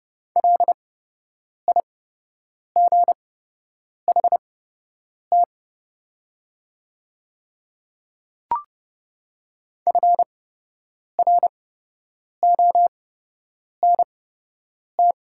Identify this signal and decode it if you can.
Morse code, heard as a keyed tone, about 30 wpm, spaced as for 8 wpm, 690 Hz. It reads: LIGHT FRONT
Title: 8x Character Spacing - Top 500 Words - 30wpm